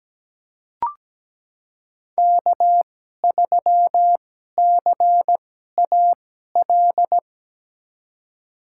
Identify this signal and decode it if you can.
Morse code, heard as a keyed tone, about 17 wpm, spaced as for 12 wpm, 700 Hz. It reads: K3CAL
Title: USA Call Signs - 5 Characters - 2x Character Spacing - 17wpm